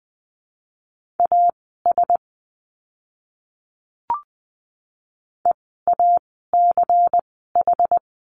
Morse code, heard as a keyed tone, about 20 wpm, spaced as for 15 wpm, 700 Hz. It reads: AS EACH
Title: 192 Lesson 31f - Letter v - Words 20wpm